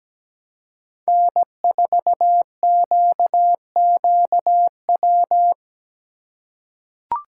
Morse code, heard as a keyed tone, about 17 wpm, 700 Hz. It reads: N4QQW
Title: USA Call Signs - 5 Characters - 3x Character Spacing - 17wpm